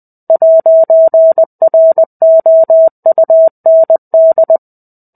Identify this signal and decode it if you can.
Morse code, heard as a keyed tone, about 20 wpm, 650 Hz. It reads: 'ROUND
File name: 05-Chapter-05 20 WPM